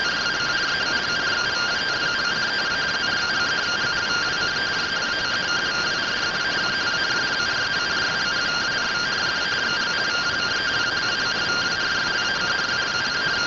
Two-tone teleprinter signal, about 170 Hz shift.
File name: findings / SAGITAL fr15 nb203 3mm.gen-08 rtty rx